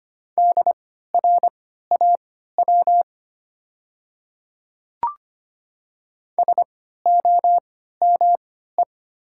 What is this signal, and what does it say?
Morse code, heard as a keyed tone, about 25 wpm, spaced as for 14 wpm, 700 Hz. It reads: DRAW SOME
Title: 3x Character Spacing - Top 300 Words - 25wpm